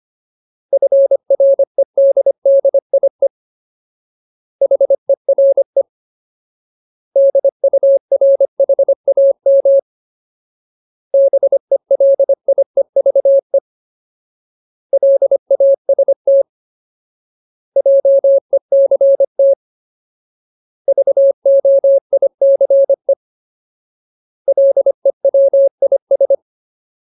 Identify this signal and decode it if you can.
Morse code, heard as a keyed tone, about 25 wpm, 550 Hz. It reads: FREDDIE HERE DURHAM BELIEVE LAST JECT VOICE LEWIS
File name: Ultimate Word List / 25WPM Ultimate